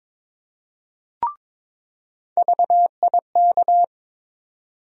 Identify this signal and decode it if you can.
Morse code, heard as a keyed tone, about 22 wpm, 700 Hz. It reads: VIK
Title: USA Call Sign Postfixes - 3 Letters 22wpm